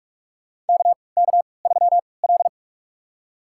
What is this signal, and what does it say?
Morse code, heard as a keyed tone, about 47 wpm, spaced as for 26 wpm, 700 Hz. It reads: KK3L
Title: USA Call Signs - 4 Characters - 3x Character Spacing - 45wpm